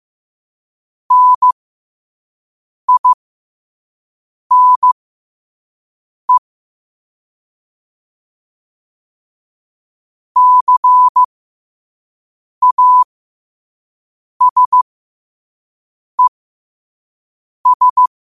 Morse code, heard as a keyed tone, about 15 wpm, spaced as for 5 wpm, 1000 Hz. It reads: NINE CASES
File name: qotdfiles / 5wpm